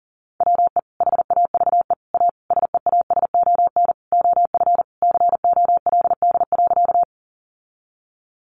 Morse code, heard as a keyed tone, about 40 wpm, 700 Hz. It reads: WE HAVE A SEASON OF COLD.
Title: Sentences from Top 1400 Words 40wpm